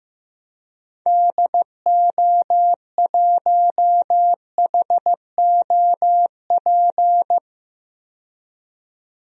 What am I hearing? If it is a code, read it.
Morse code, heard as a keyed tone, about 15 wpm, 700 Hz. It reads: DO1HOP
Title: DXCC-100 Call Signs - 6 Characters - 15wpm